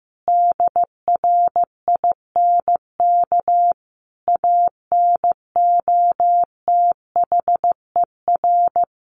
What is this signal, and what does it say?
Morse code, heard as a keyed tone, about 15 wpm, 700 Hz. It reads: DRINK ANOTHER
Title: Sentences from Top 1500 Words 15wpm